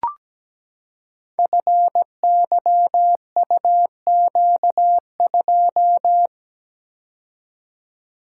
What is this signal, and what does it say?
Morse code, heard as a keyed tone, about 17 wpm, 700 Hz. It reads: FYUQ2
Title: Code Groups - 5 Letter:Numbers - 17wpm